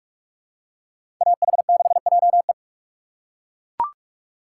Morse code, heard as a keyed tone, about 43 wpm, 690 Hz. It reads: AH6JE
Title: USA Call Signs - 8x Character Spacing - 45wpm